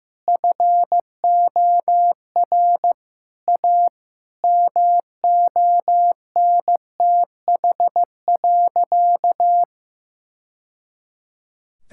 Morse code, heard as a keyed tone, about 15 wpm, 700 Hz. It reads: FOR A MONTH.
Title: Sentences from Top 1000 Words - Review Part 5 - 15wpm